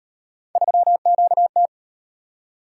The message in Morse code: 3QT

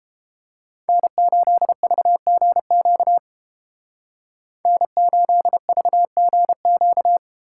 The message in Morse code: N8VGQ N8VGQ

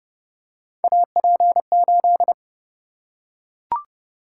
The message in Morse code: AP8

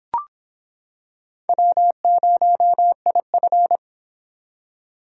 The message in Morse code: W0IF

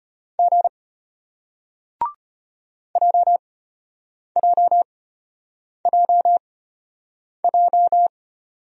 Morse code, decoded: G JJJJ

